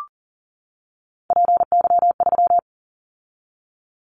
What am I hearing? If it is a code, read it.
Morse code, heard as a keyed tone, about 40 wpm, 700 Hz. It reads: PY3
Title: Callsign Prefixes for Top 100 DXCC - 40wpm